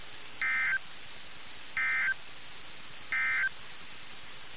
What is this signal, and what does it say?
Emergency Alert System (SAME) tones, heard as harsh buzzing bursts. It is heard from 0.3 to 3.6 s.
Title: NOAA Weather Radio - KIH61